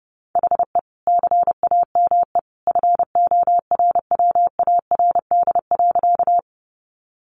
Morse code, heard as a keyed tone, about 30 wpm, 700 Hz. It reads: HE CAME FORWARD.
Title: Sentences from Top 1300 Words 30wpm